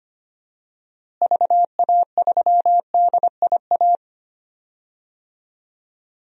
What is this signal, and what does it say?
Morse code, heard as a keyed tone, about 25 wpm, 700 Hz. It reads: VA3DIA
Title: DXCC-100 Call Signs - 6 Characters - Repeated 2x - 25wpm